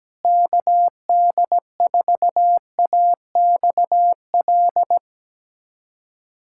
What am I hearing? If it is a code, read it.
Morse code, heard as a keyed tone, about 17 wpm, 700 Hz. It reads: KD4AXL
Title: USA Callsign DE Callsign - 6 Characters - 17wpm